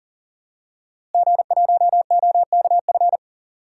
Morse code, heard as a keyed tone, about 40 wpm, 700 Hz. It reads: G1OKF
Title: DXCC-100 Call Signs - 5 Characters - Repeated 2x - 40wpm